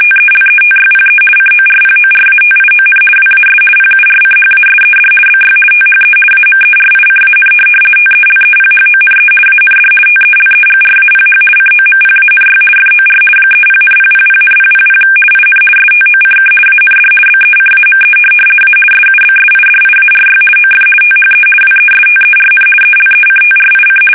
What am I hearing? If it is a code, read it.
RTTY, about 50 baud. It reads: TELETYPE CORPORATION WAS PURCHASED BY THE AMERICAN TELEPHONE AND TELEGRAPH COMPANY FOR 30,000,000 IN STOCK AND BECAME A SUBSIDIARY OF THE WESTERN ELECTRIC
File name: fsk850-50B / teletype-corp